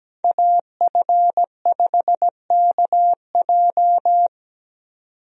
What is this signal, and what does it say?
Morse code, heard as a keyed tone, about 17 wpm, 700 Hz. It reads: AF5KJ
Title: USA Call Signs - 5 Characters - 17wpm